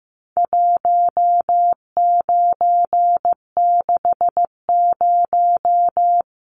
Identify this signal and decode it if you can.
Morse code, heard as a keyed tone, about 15 wpm, 700 Hz. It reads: 1960
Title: Numbers - 4-Digit 15wpm